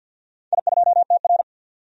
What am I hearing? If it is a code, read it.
Morse code, heard as a keyed tone, about 50 wpm, 700 Hz. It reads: I2TR
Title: DXCC-100 Call Signs - 4 Characters - 50wpm